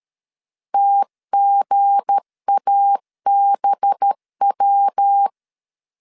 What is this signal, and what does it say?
Morse code, heard as a keyed tone, about 13 wpm, 800 Hz. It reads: TGABW
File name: LEZIONI / 7 GF9Z / 7 GF 12wpm 2122